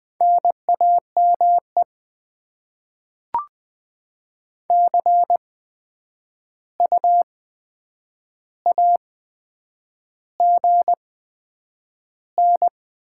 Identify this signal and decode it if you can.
Morse code, heard as a keyed tone, about 20 wpm, spaced as for 9 wpm, 700 Hz. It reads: NAME CU AGN
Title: Top 20 QSO Elements - 8x Character Spacing - 20wpm